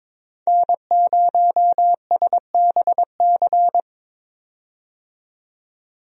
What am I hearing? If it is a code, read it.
Morse code, heard as a keyed tone, about 22 wpm, 700 Hz. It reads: N0SBC
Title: USA Call Signs - 5 Characters - 22wpm